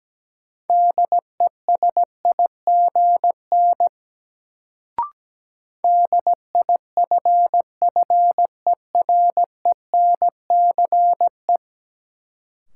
Morse code, heard as a keyed tone, about 17 wpm, 700 Hz. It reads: DESIGN DIFFERENCE
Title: Sentences from Top 800 Words - New Word Drill - 17wpm